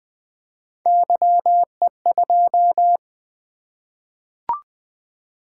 Morse code, heard as a keyed tone, about 20 wpm, 700 Hz. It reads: YE2
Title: Three Letter-Number Combinations - 20wpm